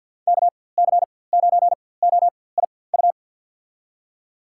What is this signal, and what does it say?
Morse code, heard as a keyed tone, about 50 wpm, spaced as for 23 wpm, 700 Hz. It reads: KC9OIU
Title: USA Call Signs - 6 Characters - 4x Character Spacing - 50wpm